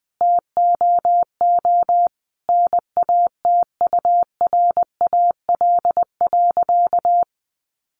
Morse code, heard as a keyed tone, about 20 wpm, 700 Hz. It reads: TOO NATURAL.
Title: Sentences from Top 700 Words 20wpm